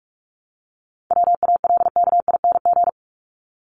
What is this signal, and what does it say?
Morse code, heard as a keyed tone, about 47 wpm, 700 Hz. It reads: WALKING
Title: ING Suffix - Top 100 Words with Base Words 45wpm